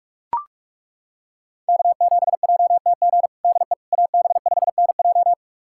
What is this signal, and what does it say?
Morse code, heard as a keyed tone, about 42 wpm, 700 Hz. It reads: K7JTG DE AB5NJ
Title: USA Callsign DE Callsign - 5 Characters - 45wpm